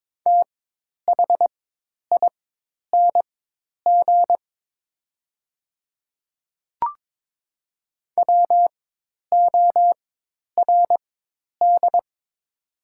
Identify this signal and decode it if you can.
Morse code, heard as a keyed tone, about 22 wpm, spaced as for 10 wpm, 700 Hz. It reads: THING WORD
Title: 4x Character Spacing - Top 100 Words - 22wpm